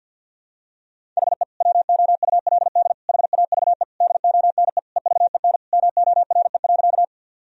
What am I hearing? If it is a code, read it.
Morse code, heard as a keyed tone, about 52 wpm, 700 Hz. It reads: HE WOULD HAVE DONE EVEN MORE.
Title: Sentences from Top 1000 Words - Review Part 1 - 50wpm